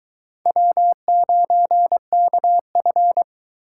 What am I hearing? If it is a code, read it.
Morse code, heard as a keyed tone, about 22 wpm, 700 Hz. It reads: W9KF